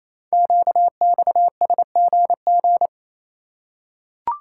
Morse code, heard as a keyed tone, about 27 wpm, 700 Hz. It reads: QXSGG
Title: Code Groups - 5 Letter:Numbers - 28wpm